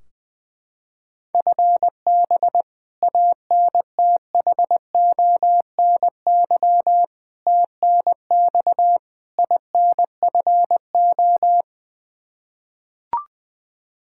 Morse code, heard as a keyed tone, about 20 wpm, 700 Hz. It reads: FB ANTHONY TNX INFO
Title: K1USN SST QSO Practice 20wpm